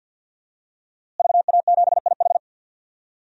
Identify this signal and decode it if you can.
Morse code, heard as a keyed tone, about 49 wpm, 700 Hz. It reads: VA7IH
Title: DXCC-100 Call Signs - 5 Characters - Repeated 2x - 50wpm